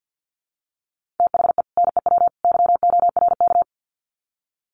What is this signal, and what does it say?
Morse code, heard as a keyed tone, about 50 wpm, 700 Hz. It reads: THE NEW YORK
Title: Sets of 3 Words 50wpm